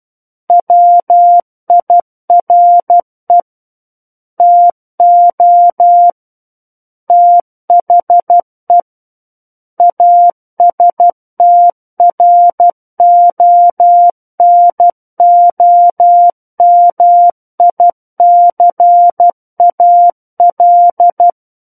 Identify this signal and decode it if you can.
Morse code, heard as a keyed tone, about 12 wpm, 700 Hz. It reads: WIRE TO THE ASTRONOMICAL
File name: Book One / 03-Chapter 3 - 12 WPM